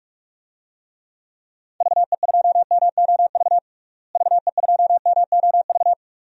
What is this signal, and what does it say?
Morse code, heard as a keyed tone, about 41 wpm, 700 Hz. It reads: VE2MOV VE2MOV
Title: DXCC-100 Call Signs - 6 Characters - Repeated 3x - 45wpm